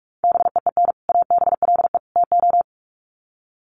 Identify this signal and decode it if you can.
Morse code, heard as a keyed tone, about 44 wpm, 700 Hz. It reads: BEEN ABLE TO
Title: Sets of 3 Words 45wpm